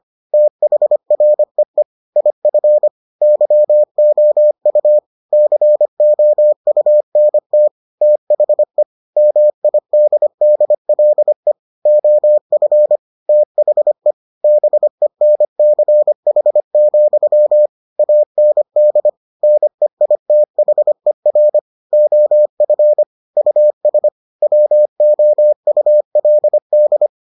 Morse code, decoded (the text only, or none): THREE IF YOU COUNT THE MIDDLE OF THE BENCH, AND NEITHER OF US WOULD